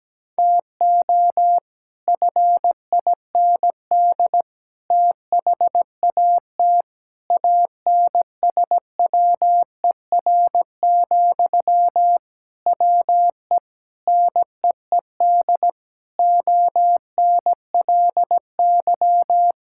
Morse code, heard as a keyed tone, about 17 wpm, 700 Hz. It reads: TO FIND THAT ANSWER, WE NEED ONLY
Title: Sentences from Top 1600 Words 17wpm